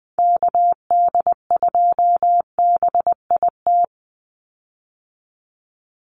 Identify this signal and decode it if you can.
Morse code, heard as a keyed tone, about 20 wpm, 700 Hz. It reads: KD2BIT
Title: USA Call Signs 20wpm